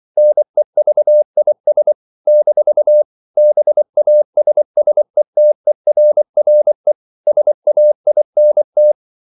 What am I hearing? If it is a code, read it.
Morse code, heard as a keyed tone, about 24 wpm, 600 Hz. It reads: NEVIS - BASSETERRE SAINT
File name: CW / Practice / NationalCapitals24